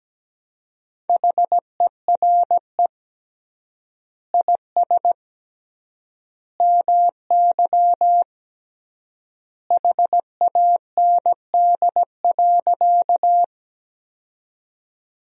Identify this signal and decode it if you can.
Morse code, heard as a keyed tone, about 17 wpm, 700 Hz. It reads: HERE IS MY HAND.
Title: 3x Word-Spacing - Sentences from Top 200 Words - 17wpm